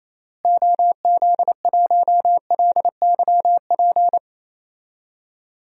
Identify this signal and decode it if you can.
Morse code, heard as a keyed tone, about 28 wpm, 700 Hz. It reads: OZ1LYP
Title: DXCC-100 Call Signs - 6 Characters - Repeated 3x - 28wpm